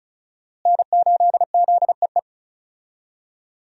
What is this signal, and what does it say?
Morse code, heard as a keyed tone, about 34 wpm, 700 Hz. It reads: N8ZEE
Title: USA Call Signs - 5 Characters - Repeated 3x - 35wpm